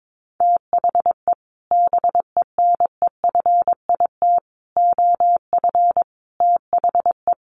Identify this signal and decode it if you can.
Morse code, heard as a keyed tone, about 22 wpm, 700 Hz. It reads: THE BENEFIT OF THE
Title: Sets of 5 Words Encore 22wpm